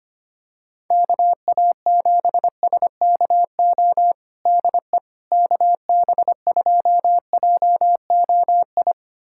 Morse code, heard as a keyed tone, about 25 wpm, 700 Hz. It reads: KA7SKO DE KB2JOI
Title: USA Callsign DE Callsign - 6 Characters - 25wpm